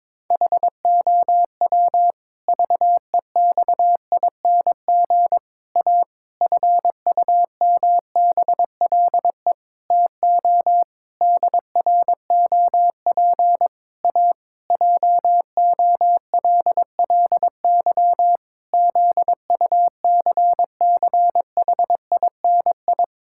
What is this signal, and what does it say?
Morse code, heard as a keyed tone, about 22 wpm, 700 Hz. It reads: HOW VEXING A FUMBLE TO DROP A JOLLY ZUCCHINI
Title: Pangram Challenge v2 - 22wpm